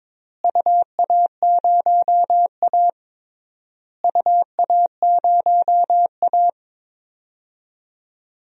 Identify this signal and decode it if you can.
Morse code, heard as a keyed tone, about 22 wpm, 700 Hz. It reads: UA0A UA0A
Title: DXCC-100 Call Signs - 4 Characters - Repeated 3x - 22wpm